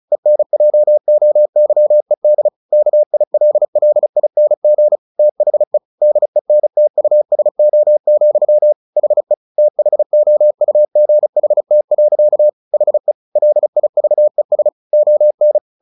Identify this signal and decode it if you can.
Morse code, heard as a keyed tone, about 34 wpm, 600 Hz. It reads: ENJOYED KILLING THE DENTUSO, HE THOUGHT. HE LIVES ON
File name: the old man and the sea 2 35wpm